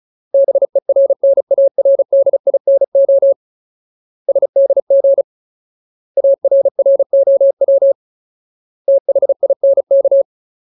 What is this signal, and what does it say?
Morse code, heard as a keyed tone, about 35 wpm, 550 Hz. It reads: BERNARDINO SDG ARROW THINK
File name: Ultimate Word List / 35WPM Ultimate